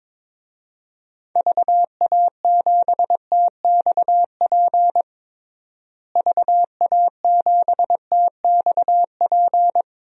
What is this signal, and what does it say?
Morse code, heard as a keyed tone, about 22 wpm, 700 Hz. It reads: VA7TXP VA7TXP
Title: DXCC-100 Call Signs - 6 Characters - Repeated 2x - 22wpm